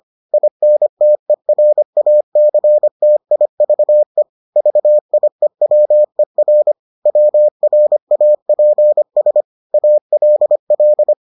Morse code, heard as a keyed tone, about 25 wpm, 600 Hz. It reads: INTERACTIVE VIEWER WRAPS ALL